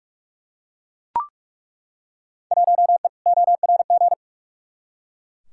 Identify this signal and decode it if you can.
Morse code, heard as a keyed tone, about 41 wpm, 700 Hz. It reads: 1E ORG
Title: Field Day Exchanges - 45wpm